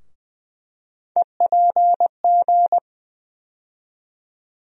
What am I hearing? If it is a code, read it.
Morse code, heard as a keyed tone, about 20 wpm, 700 Hz. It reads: EPG
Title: USA Call Sign Postfixes - 3 Letters 20wpm